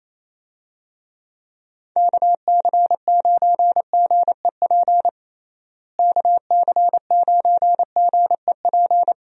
Morse code, heard as a keyed tone, about 28 wpm, 700 Hz. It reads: KC9GEP KC9GEP
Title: USA Call Signs - 6 Characters - Repeated 2x - 28wpm